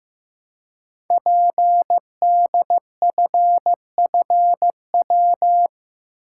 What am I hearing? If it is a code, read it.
Morse code, heard as a keyed tone, about 15 wpm, 700 Hz. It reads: PDFFW
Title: Code Groups - 5 Letter:Numbers - 15wpm